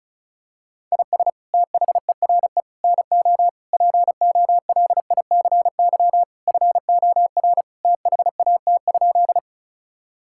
Morse code, heard as a keyed tone, about 34 wpm, 700 Hz. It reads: IS THERE NO POLICY FOR THAT?
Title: Sentences from Top 1000 Words - Review Part 5 - 35wpm